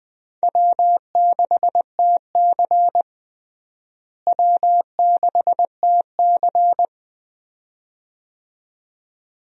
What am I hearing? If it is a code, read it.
Morse code, heard as a keyed tone, about 20 wpm, 700 Hz. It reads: W6TC W6TC
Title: USA Call Signs - 4 Characters - Repeated 2x - 20wpm